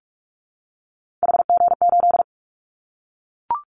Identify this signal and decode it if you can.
Morse code, heard as a keyed tone, about 44 wpm, 700 Hz. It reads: HG8
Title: Callsign Prefixes for Top 25 DXCC - Repeated 3x - 45wpm